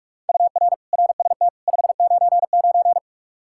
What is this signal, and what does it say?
Morse code, heard as a keyed tone, about 44 wpm, 700 Hz. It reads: UR RST 599
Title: K1USN SST QSO Practice 45wpm